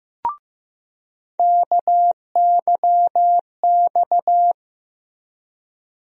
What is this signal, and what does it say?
Morse code, heard as a keyed tone, about 15 wpm, 700 Hz. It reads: KYX
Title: USA Call Sign Postfixes - 3 Letters 15wpm